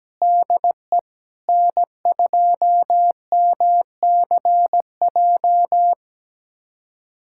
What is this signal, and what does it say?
Morse code, heard as a keyed tone, about 17 wpm, 700 Hz. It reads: DE N2MCJ
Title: USA Callsign DE Callsign 17wpm